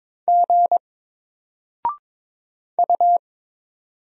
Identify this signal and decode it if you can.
Morse code, heard as a keyed tone, about 22 wpm, 700 Hz. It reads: G U
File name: Single-Letter-Number-22wpm